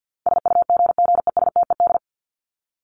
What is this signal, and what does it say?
Morse code, heard as a keyed tone, about 50 wpm, 700 Hz. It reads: SUGGESTED
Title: ED Suffix - Top 100 Words with Base Words 50wpm